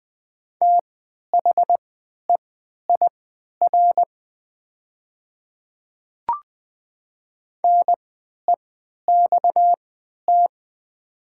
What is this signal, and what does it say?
Morse code, heard as a keyed tone, about 20 wpm, spaced as for 11 wpm, 700 Hz. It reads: THEIR NEXT
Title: 3x Character Spacing - Top 300 Words - 20wpm